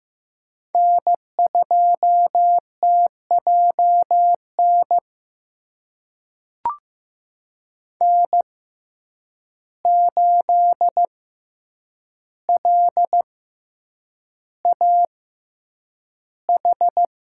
Morse code, heard as a keyed tone, about 15 wpm, spaced as for 8 wpm, 700 Hz. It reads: N2TJN N8LAH
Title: USA Call Signs - 6x Character Spacing - 15wpm